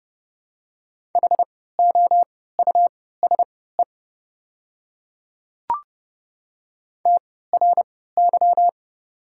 Morse code, heard as a keyed tone, about 30 wpm, spaced as for 17 wpm, 700 Hz. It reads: HOUSE TRY
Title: 3x Character Spacing - Top 200 Words - 30wpm